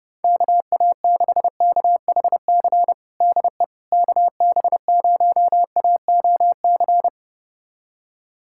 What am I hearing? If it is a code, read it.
Morse code, heard as a keyed tone, about 30 wpm, 700 Hz. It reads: KA6KHC DE KB0AOC
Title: USA Callsign DE Callsign - 6 Characters - 30wpm